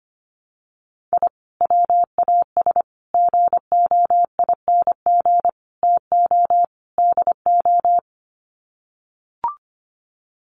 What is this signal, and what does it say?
Morse code, heard as a keyed tone, about 25 wpm, 700 Hz. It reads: I WAS GOING TO DO
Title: Sets of 5 Words Encore2 25wpm